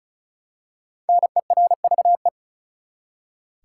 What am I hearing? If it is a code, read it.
Morse code, heard as a keyed tone, about 34 wpm, 700 Hz. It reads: NERVE
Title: Five Letter Words - 35wpm